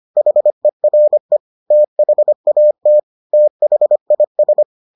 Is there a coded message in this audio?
Yes: Morse code from 0.2 to 4.6 s, heard as a keyed tone, about 25 wpm, 600 Hz. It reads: HERE THAT THIS